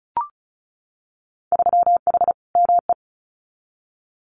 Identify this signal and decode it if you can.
Morse code, heard as a keyed tone, about 36 wpm, 700 Hz. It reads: 3H ME
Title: Winter Field Day Exchanges Encore 35wpm